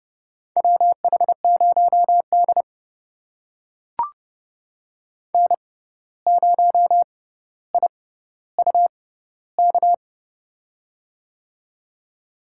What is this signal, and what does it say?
Morse code, heard as a keyed tone, about 28 wpm, spaced as for 14 wpm, 700 Hz. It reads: WH0D N0IUK
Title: USA Call Signs - 6x Character Spacing - 30wpm